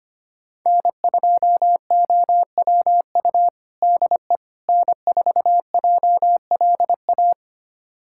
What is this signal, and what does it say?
Morse code, heard as a keyed tone, about 25 wpm, 700 Hz. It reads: N2OWU DE N4JLA
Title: USA Callsign DE Callsign - 5 Characters - 25wpm